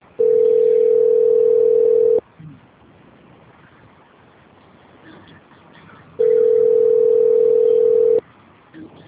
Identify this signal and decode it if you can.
Ringback tone (North America), 440 + 480 Hz.